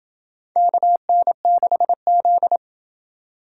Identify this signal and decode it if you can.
Morse code, heard as a keyed tone, about 26 wpm, 700 Hz. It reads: KN6Z